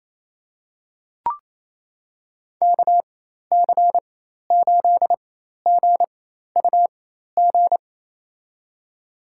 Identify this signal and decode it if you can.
Morse code, heard as a keyed tone, about 28 wpm, spaced as for 13 wpm, 700 Hz. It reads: KC8GUG
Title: USA Call Signs - 4x Character Spacing - 28wpm